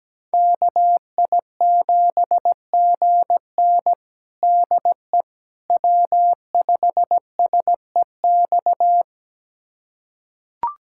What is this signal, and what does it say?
Morse code, heard as a keyed tone, about 17 wpm, 700 Hz. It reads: KI7GN DE W5SEX
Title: USA Callsign DE Callsign - 5 Characters - 17wpm